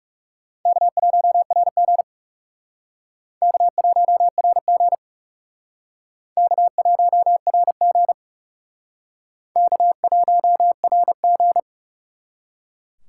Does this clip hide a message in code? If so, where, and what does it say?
Yes: Morse code from 0.7 to 11.6 s, heard as a keyed tone, about 40 wpm, 700 Hz. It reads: K1RG K1RG K1RG K1RG